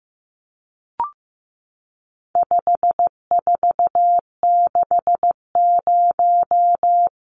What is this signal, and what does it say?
Morse code, heard as a keyed tone, about 14 wpm, 700 Hz. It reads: 5460
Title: Numbers - 4-Digit 15wpm